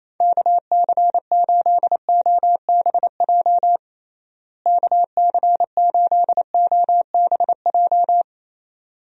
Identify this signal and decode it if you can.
Morse code, heard as a keyed tone, about 27 wpm, 700 Hz. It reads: KC8OBJ KC8OBJ
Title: USA Call Signs - 6 Characters - Repeated 2x - 28wpm